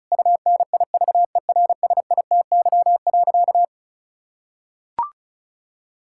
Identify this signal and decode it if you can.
Morse code, heard as a keyed tone, about 36 wpm, 700 Hz. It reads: UNIVERSITY.
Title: Sentences from Top 1000 Words - Review Part 5 - 35wpm